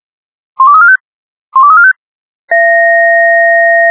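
Five-tone selective call (ZVEI), heard as quick runs of five tones, tones of about 70 ms. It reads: 12345 12345
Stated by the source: ZVEI five-tone call received by SDR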